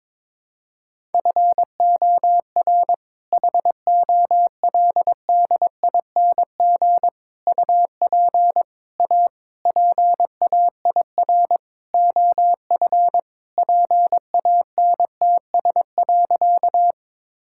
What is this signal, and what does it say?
Morse code, heard as a keyed tone, about 22 wpm, 700 Hz. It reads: FOR HOLDING UP A PAIR OF PANTS.
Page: morse-code-ninja-practice-22wpm